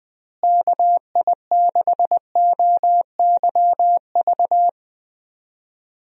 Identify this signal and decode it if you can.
Morse code, heard as a keyed tone, about 20 wpm, 700 Hz. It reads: KI6OYV
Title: USA Call Signs - 6 Characters - Repeated 2x - 20wpm